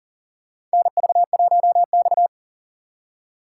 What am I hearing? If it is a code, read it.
Morse code, heard as a keyed tone, about 40 wpm, 700 Hz. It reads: NV1X
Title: USA Call Signs - 4 Characters - 4x Character Spacing - 40wpm